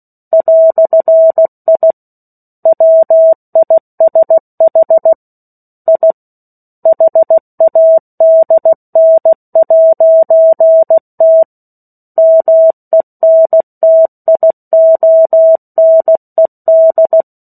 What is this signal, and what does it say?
Morse code, heard as a keyed tone, about 16 wpm, 650 Hz. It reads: "I WISH I HADN'T MENTIONED
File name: 03-Chapter-03 16 WPM